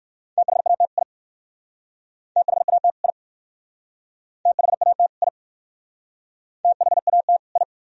Morse code, heard as a keyed tone, about 56 wpm, 700 Hz. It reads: THAT I THAT I THAT I THAT I
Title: Speed-Racing - Sets of 2 Words 45wpm